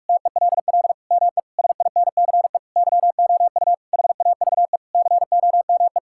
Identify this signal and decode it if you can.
Morse code, heard as a keyed tone, about 41 wpm, 700 Hz. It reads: TELL ME SINCE YOU HAVE COME
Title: Sentences from Top 1600 Words 45wpm